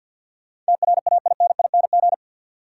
Morse code, heard as a keyed tone, about 53 wpm, 700 Hz. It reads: TRAINING
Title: Sentences from Top 1000 Words - Review Part 4 - New Word Drill - 50wpm